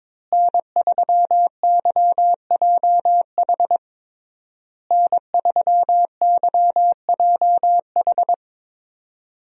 Morse code, heard as a keyed tone, about 22 wpm, 700 Hz. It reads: N3YJH N3YJH
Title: USA Call Signs - 5 Characters - Repeated 3x - 22wpm